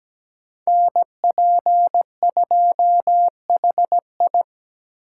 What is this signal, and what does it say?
Morse code, heard as a keyed tone, about 20 wpm, 700 Hz. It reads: NP2HI
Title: Speed Racing - USA 2x2 Call Signs 17wpm